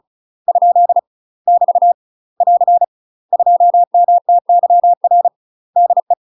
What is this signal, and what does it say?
Morse code, heard as a keyed tone, about 34 wpm, 700 Hz. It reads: ? <BT> <AR> 2MTYR DE
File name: UK MASTER QSOs TRAINING / 35WPM / UK MASTER QSOs 0042